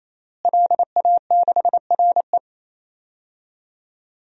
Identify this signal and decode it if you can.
Morse code, heard as a keyed tone, about 28 wpm, 700 Hz. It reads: LA6RE
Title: DXCC-100 Call Signs - 5 Characters - Repeated 3x - 28wpm